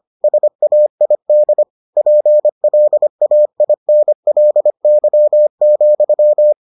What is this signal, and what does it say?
Morse code, heard as a keyed tone, about 25 wpm, 600 Hz. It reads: SAID PLAINLY,